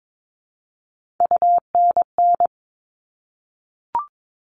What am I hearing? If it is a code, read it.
Morse code, heard as a keyed tone, about 22 wpm, 700 Hz. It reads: UNN
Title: All Cut Numbers - 3-Digit 22wpm